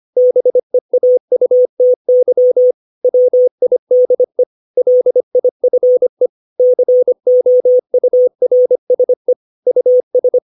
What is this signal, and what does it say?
Morse code, heard as a keyed tone, about 25 wpm, 500 Hz. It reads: BEAUTY WIDE LIFE COURSE US
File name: 632 words / 632 Words 25WPM